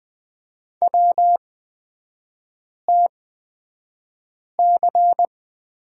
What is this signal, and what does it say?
Morse code, heard as a keyed tone, about 20 wpm, spaced as for 5 wpm, 700 Hz. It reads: WTC
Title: 250 Lesson 40e - BK - Three Characters 20wpm